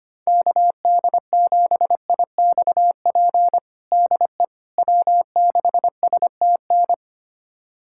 Morse code, heard as a keyed tone, about 25 wpm, 700 Hz. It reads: KD7IXP DE W6STN
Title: USA Callsign DE Callsign 25wpm